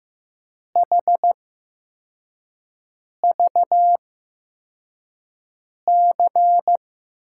Morse code, heard as a keyed tone, about 15 wpm, spaced as for 4 wpm, 700 Hz. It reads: HVC
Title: USA Call Signs - 5 Characters - 8x Character Spacing - 15wpm